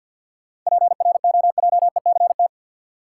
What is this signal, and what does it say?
Morse code, heard as a keyed tone, about 48 wpm, 700 Hz. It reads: PROJECT
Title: Sentences from Top 1400 Words - New Word Drill Alt - 50wpm